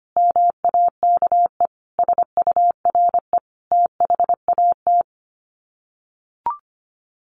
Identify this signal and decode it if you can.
Morse code, heard as a keyed tone, about 25 wpm, 700 Hz. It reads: MAKE SURE THAT
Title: Sets of 4 Words 25wpm